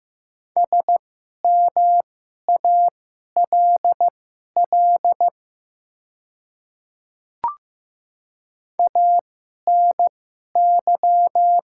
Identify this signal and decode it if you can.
Morse code, heard as a keyed tone, about 15 wpm, spaced as for 11 wpm, 700 Hz. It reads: SMALL ANY
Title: 2x Character Spacing - Top 200 Words - 15wpm